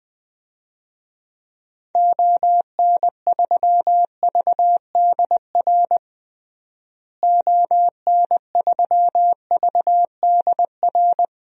Morse code, heard as a keyed tone, about 21 wpm, 700 Hz. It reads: ON3VDR ON3VDR
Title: DXCC-100 Call Signs - 6 Characters - Repeated 2x - 20wpm